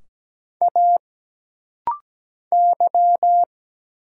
Morse code, heard as a keyed tone, about 17 wpm, 700 Hz. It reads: A Y
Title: Single Letters - Rapid-Fire 17wpm